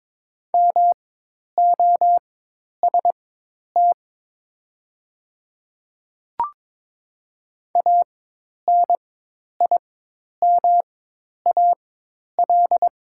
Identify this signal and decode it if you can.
Morse code, heard as a keyed tone, about 22 wpm, spaced as for 10 wpm, 700 Hz. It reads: MOST ANIMAL
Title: 4x Character Spacing - Top 500 Words - 22wpm